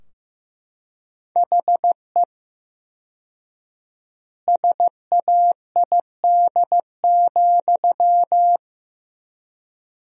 Morse code, heard as a keyed tone, about 15 wpm, 700 Hz. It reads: HE SAID,
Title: 4x Word-Spacing - Sentences from Top 400 Words - 15wpm